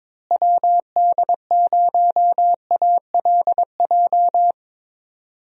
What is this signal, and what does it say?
Morse code, heard as a keyed tone, about 22 wpm, 700 Hz. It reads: WD0ALJ
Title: USA Call Signs - 6 Characters - Repeated 3x - 22wpm